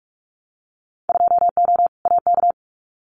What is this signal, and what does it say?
Morse code, heard as a keyed tone, about 44 wpm, 700 Hz. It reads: 2O AK
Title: Winter Field Day Exchanges Encore 45wpm